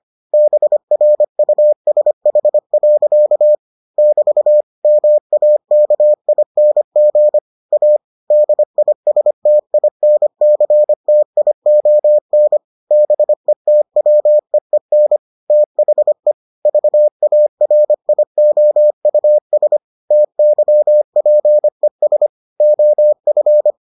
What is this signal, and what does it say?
Morse code, heard as a keyed tone, about 25 wpm, 600 Hz. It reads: BRUSH. <BT> MAKING A DISTINCTION BETWEEN THE VARIOUS TYPES OF